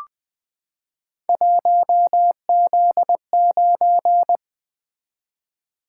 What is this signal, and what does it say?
Morse code, heard as a keyed tone, about 20 wpm, 700 Hz. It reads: 1Z9
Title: Three Letter-Number Combinations - 20wpm